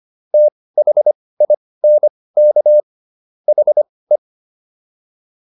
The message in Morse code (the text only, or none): THINK HE